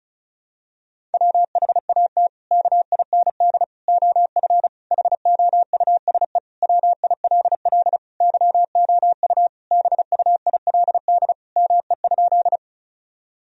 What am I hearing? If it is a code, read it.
Morse code, heard as a keyed tone, about 36 wpm, 700 Hz. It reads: WHAT KIND OF HOUSE WILL YOU BUILD ME?
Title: Sentences from Top 1000 Words - Review Part 1 - 35wpm